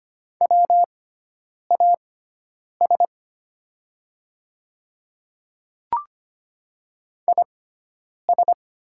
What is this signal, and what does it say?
Morse code, heard as a keyed tone, about 25 wpm, spaced as for 9 wpm, 700 Hz. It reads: WAS IS
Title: 6x Character Spacing - Top 100 Words - 25wpm